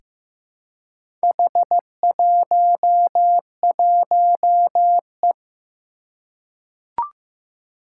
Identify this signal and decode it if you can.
Morse code, heard as a keyed tone, about 15 wpm, 700 Hz. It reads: H11E
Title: Four Challenging Letter-Number Combinations 15wpm